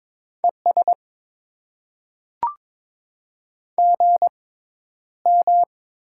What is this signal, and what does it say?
Morse code, heard as a keyed tone, about 22 wpm, spaced as for 11 wpm, 690 Hz. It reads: ES GM
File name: Top 20 QSO Elements - 6x Character Spacing - 22wpm